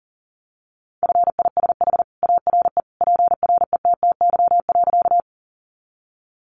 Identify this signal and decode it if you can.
Morse code, heard as a keyed tone, about 40 wpm, 700 Hz. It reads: FISH ARE PRETTY.
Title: Easy Phrases 40wpm